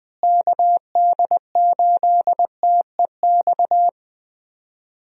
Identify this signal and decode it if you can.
Morse code, heard as a keyed tone, about 20 wpm, 700 Hz. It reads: KD8TEX
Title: USA Call Signs - 6 Characters - Repeated 3x - 20wpm